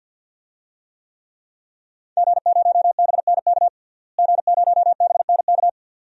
Morse code, heard as a keyed tone, about 52 wpm, 700 Hz. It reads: G0BNK G0BNK
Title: DXCC-100 Call Signs - 5 Characters - Repeated 2x - 50wpm